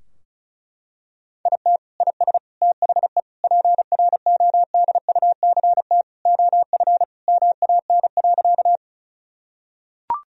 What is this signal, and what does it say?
Morse code, heard as a keyed tone, about 36 wpm, 700 Hz. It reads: IT IS THE PRODUCT OF MAN.
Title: Sentences from Top 1000 Words - Review Part 2 - 35wpm